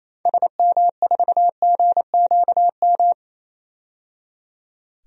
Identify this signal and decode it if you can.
Morse code, heard as a keyed tone, about 28 wpm, 700 Hz. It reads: SM4GQM
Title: DXCC-100 Call Signs - 6 Characters - 28wpm